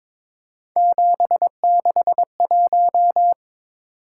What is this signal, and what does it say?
Morse code, heard as a keyed tone, about 22 wpm, 700 Hz. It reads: 761
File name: practice / 22wpm / Three Letter-Number Combinations - 22wpm